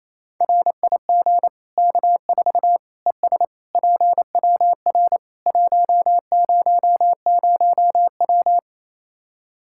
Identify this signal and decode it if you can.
Morse code, heard as a keyed tone, about 28 wpm, 700 Hz. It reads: RIG K4 ES PWR 100W
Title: Practice QSO - Exchange 2 - Info - 28wpm